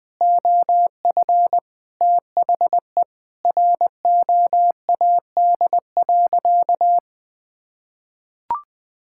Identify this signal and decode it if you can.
Morse code, heard as a keyed tone, about 20 wpm, 700 Hz. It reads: OF THE ROAD.
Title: Sentences with 4-Word Phrases - 20wpm